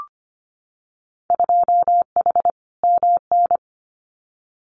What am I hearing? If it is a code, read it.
Morse code, heard as a keyed tone, about 25 wpm, 700 Hz. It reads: 2H MN
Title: Winter Field Day Exchanges 25wpm